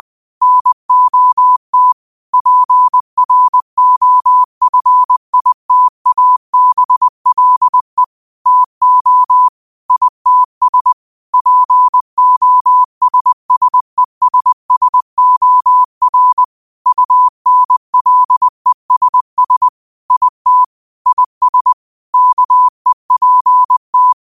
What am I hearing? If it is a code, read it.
Morse code, heard as a keyed tone, about 20 wpm, 1000 Hz. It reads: NOT PROFITABLE TO ITS POSSESSOR UNLESS IT IS KEPT
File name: qotdfiles / 20wpm